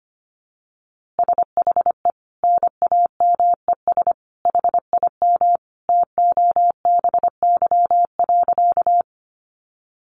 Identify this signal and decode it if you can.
Morse code, heard as a keyed tone, about 25 wpm, 700 Hz. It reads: SHE NAMES HIM TOBY.
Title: Easy Phrases 25wpm